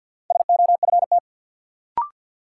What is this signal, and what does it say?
Morse code, heard as a keyed tone, about 52 wpm, 700 Hz. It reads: SOFT
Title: Sentences from Top 1500 Words - New Word Drill - 50wpm